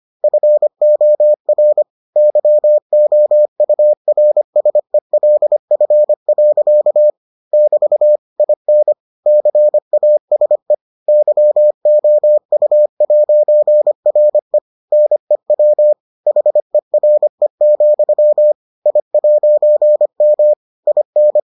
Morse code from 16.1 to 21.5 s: HERE, I'M IN

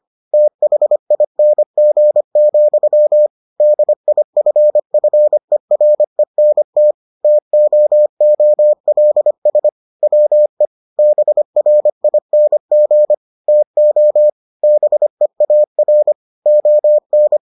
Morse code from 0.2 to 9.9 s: THING, DIFFERENT TOOLS

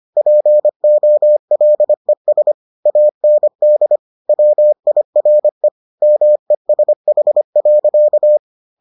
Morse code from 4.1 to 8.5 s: WIRE MESH.

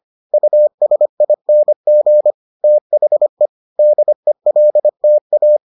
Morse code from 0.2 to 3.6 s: USING THE

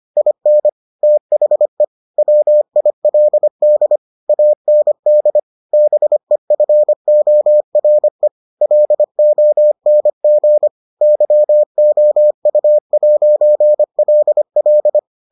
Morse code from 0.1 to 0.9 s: IN

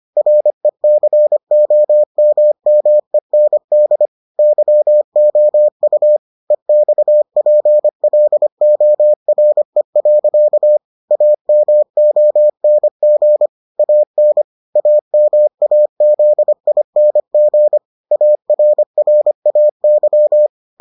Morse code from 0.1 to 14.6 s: RECOMMEND YOU EXPLORE. AMONG AN

